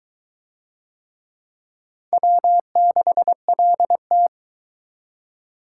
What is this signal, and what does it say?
Morse code, heard as a keyed tone, about 20 wpm, 700 Hz. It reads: W6LT